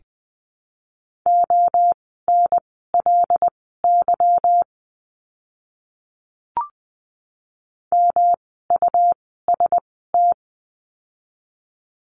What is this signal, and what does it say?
Morse code, heard as a keyed tone, about 20 wpm, spaced as for 14 wpm, 700 Hz. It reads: ONLY MUST
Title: 237 Lesson 38f - Letter q - Words 20wpm